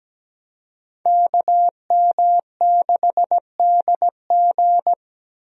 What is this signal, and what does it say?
Morse code, heard as a keyed tone, about 17 wpm, 700 Hz. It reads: KM6DG
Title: USA Call Signs - 5 Characters - 17wpm